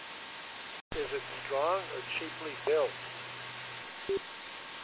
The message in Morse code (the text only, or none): none